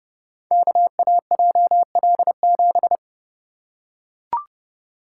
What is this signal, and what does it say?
Morse code, heard as a keyed tone, about 29 wpm, 700 Hz. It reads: KAJL7
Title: Code Groups - 5 Letter:Numbers - 30wpm